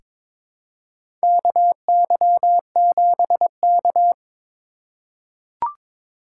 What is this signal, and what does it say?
Morse code, heard as a keyed tone, about 22 wpm, 700 Hz. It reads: KY7K
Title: K1USN SST QSO Practice - 22wpm